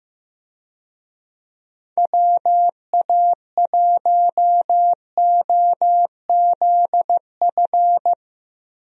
Morse code, heard as a keyed tone, about 15 wpm, 700 Hz. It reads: WA1OZF
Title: USA Call Signs - 6 Characters - Repeated 3x - 15wpm